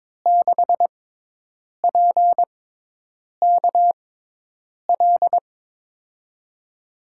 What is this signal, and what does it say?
Morse code, heard as a keyed tone, about 22 wpm, spaced as for 8 wpm, 700 Hz. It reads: 6PKL
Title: USA Call Signs - 6 Characters - 6x Character Spacing - 22wpm